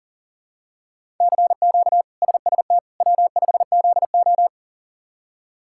Morse code, heard as a keyed tone, about 40 wpm, 700 Hz. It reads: CQ SST W5ZO
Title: K1USN SST QSO Practice - 40wpm